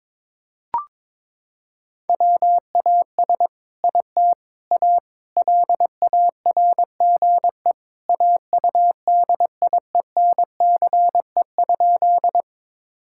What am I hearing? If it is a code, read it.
Morse code, heard as a keyed tone, about 22 wpm, 700 Hz. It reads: WAS IT A LARGE AUDIENCE?
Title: Sentences from Top 1000 Words - Review Part 5 - 22wpm